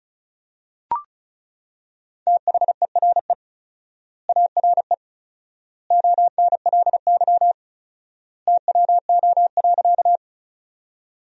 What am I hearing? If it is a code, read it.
Morse code, heard as a keyed tone, about 34 wpm, 700 Hz. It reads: THERE ARE ONLY TWO.
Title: 4x Word-Spacing - Sentences from Top 200 Words - 35wpm